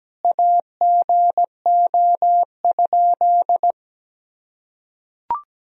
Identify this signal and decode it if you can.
Morse code, heard as a keyed tone, about 17 wpm, 700 Hz. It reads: AGO?
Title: Sentences from Top 500 Words and Abbreviations 17wpm